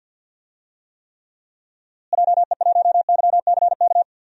Morse code, heard as a keyed tone, about 52 wpm, 700 Hz. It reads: JE1YCK
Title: DXCC-100 Call Signs - 6 Characters - Repeated 3x - 50wpm